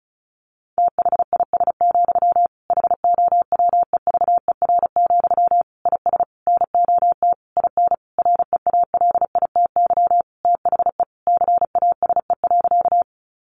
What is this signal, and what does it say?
Morse code, heard as a keyed tone, about 34 wpm, 700 Hz. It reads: THIS, HOWEVER, IS NOT IN REALITY THE CASE.
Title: Sentences from Top 1200 Words 35wpm